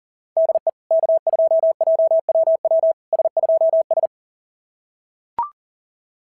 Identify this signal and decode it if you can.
Morse code, heard as a keyed tone, about 40 wpm, 690 Hz. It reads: DE K2JWW S2S
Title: SOTA QSO Practice 40wpm